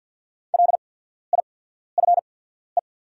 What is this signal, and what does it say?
Morse code, heard as a keyed tone, about 48 wpm, spaced as for 14 wpm, 700 Hz. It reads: LIFE
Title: 8x Character Spacing - Top 300 Words - 50wpm